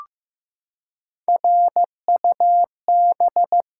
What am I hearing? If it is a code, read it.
Morse code, heard as a keyed tone, about 15 wpm, 700 Hz. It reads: RUB
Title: Three Letter Words - 15wpm